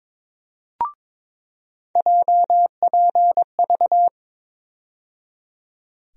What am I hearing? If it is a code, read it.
Morse code, heard as a keyed tone, about 22 wpm, 700 Hz. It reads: JPV
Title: USA Call Sign Postfixes - 3 Letters 22wpm